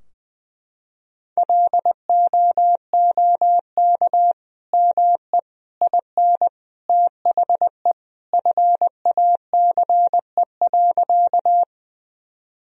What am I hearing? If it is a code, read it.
Morse code, heard as a keyed tone, about 20 wpm, 700 Hz. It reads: LOOK ME IN THE FACE.
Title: Sentences from Top 1000 Words - Review Part 2 - 20wpm